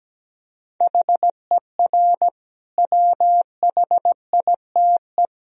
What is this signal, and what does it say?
Morse code, heard as a keyed tone, about 17 wpm, 700 Hz. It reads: HER WHITE
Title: Sentences from Top 1000 Words - Review Part 1 - 17wpm